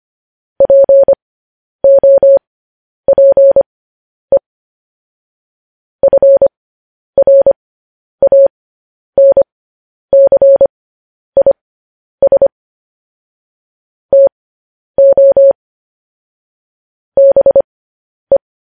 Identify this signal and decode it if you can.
Morse code, heard as a keyed tone, about 25 wpm, spaced as for 10 wpm, 550 Hz. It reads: POPE FRANCIS TO BE
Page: news-headlines-in-morse-code-at-10-wpm